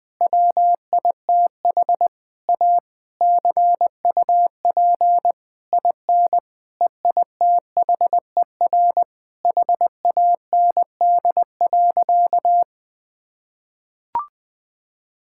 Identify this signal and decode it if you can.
Morse code, heard as a keyed tone, about 20 wpm, 700 Hz. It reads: WITH A CUP IN EITHER HAND.
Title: Sentences from Top 1600 Words 20wpm